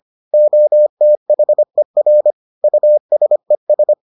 Morse code, heard as a keyed tone, about 25 wpm, 600 Hz. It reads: OTHER USES